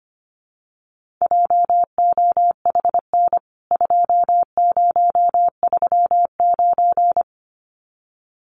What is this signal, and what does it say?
Morse code, heard as a keyed tone, about 25 wpm, 700 Hz. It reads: JOHN 2039
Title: CWT Exchange 25wpm